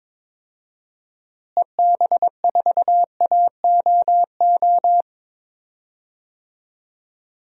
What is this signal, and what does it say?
Morse code, heard as a keyed tone, about 22 wpm, 700 Hz. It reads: EB4AOO
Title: DXCC-100 Call Signs - 6 Characters - Repeated 3x - 22wpm